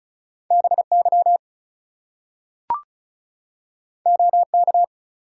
Morse code, heard as a keyed tone, about 36 wpm, 700 Hz. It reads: BY OK
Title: Two Letter Words - 35wpm